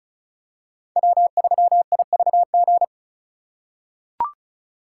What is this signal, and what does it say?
Morse code, heard as a keyed tone, about 34 wpm, 700 Hz. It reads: W3IVG
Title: USA Call Signs - 5 Characters - 3x Character Spacing - 35wpm